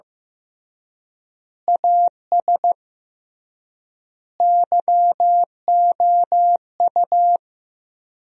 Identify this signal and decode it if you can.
Morse code, heard as a keyed tone, about 15 wpm, 700 Hz. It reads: AS YOU
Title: 3x Word-Spacing - Sentences from Top 400 Words - 15wpm